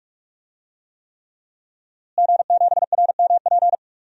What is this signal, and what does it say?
Morse code, heard as a keyed tone, about 46 wpm, 700 Hz. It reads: G7RMP
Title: DXCC-100 Call Signs - 5 Characters - Repeated 3x - 45wpm